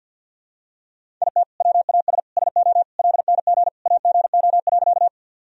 Morse code, heard as a keyed tone, about 52 wpm, 700 Hz. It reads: IT WAS SO LNG AGO.
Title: Sentences from Top 500 Words and Abbreviations 50wpm